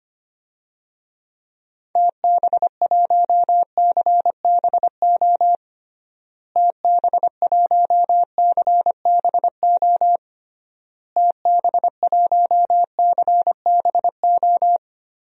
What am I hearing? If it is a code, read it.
Morse code, heard as a keyed tone, about 25 wpm, 700 Hz. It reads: TB1CBO TB1CBO TB1CBO
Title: DXCC-100 Call Signs - 6 Characters - Repeated 3x - 25wpm